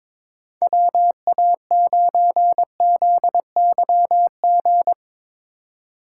Morse code, heard as a keyed tone, about 22 wpm, 700 Hz. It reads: WA9ZYG
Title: USA Call Signs - 6 Characters - 4x Character Spacing - 22wpm